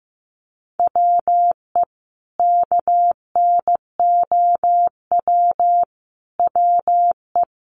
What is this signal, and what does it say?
Morse code, heard as a keyed tone, about 15 wpm, 700 Hz. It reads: WE KNOW WE
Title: Sentences from Top 1100 Words 15wpm